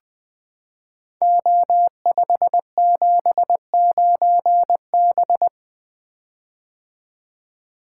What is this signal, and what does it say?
Morse code, heard as a keyed tone, about 20 wpm, 700 Hz. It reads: O579B
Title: Code Groups - 5 Letter:Numbers - 20wpm